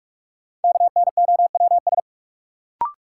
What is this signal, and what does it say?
Morse code, heard as a keyed tone, about 46 wpm, 700 Hz. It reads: KNOWS
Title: Sentences from Top 1000 Words - Review Part 1 - New Word Drill - 45wpm